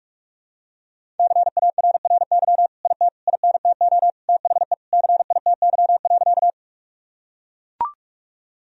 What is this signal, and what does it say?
Morse code, heard as a keyed tone, about 41 wpm, 700 Hz. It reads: CARRY IT INTO THE CITY.
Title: Sentences from Top 1000 Words - Review Part 1 - 45wpm